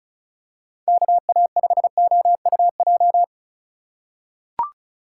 Morse code, heard as a keyed tone, about 36 wpm, 700 Hz. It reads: KA5OUJ
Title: USA Call Signs - 4x Character Spacing - 35wpm